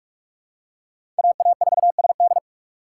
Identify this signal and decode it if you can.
Morse code, heard as a keyed tone, about 48 wpm, 700 Hz. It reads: A A 4 S D
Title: USA Call Signs - 5 Characters - 6x Character Spacing - 45wpm